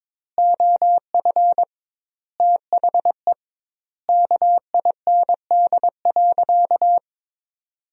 Morse code, heard as a keyed tone, about 22 wpm, 700 Hz. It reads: OF THE KIND.
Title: 2x Word-Spacing - Sentences from Top 500 Words - 22wpm